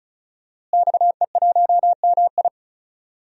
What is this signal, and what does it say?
Morse code, heard as a keyed tone, about 36 wpm, 700 Hz. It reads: XE1MI
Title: DXCC-100 Call Signs - 5 Characters - Repeated 2x - 35wpm